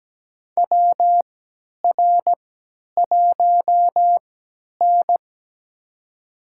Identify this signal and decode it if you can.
Morse code, heard as a keyed tone, about 17 wpm, spaced as for 10 wpm, 700 Hz. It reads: WR1N
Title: USA Call Signs - 4 Characters - 3x Character Spacing - 17wpm